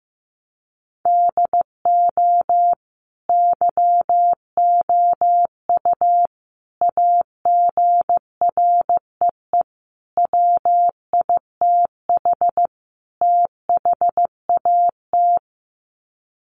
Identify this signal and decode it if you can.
Morse code, heard as a keyed tone, about 14 wpm, 700 Hz. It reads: DO YOU AGREE WITH THAT
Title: Sets of 5 Words Encore 15wpm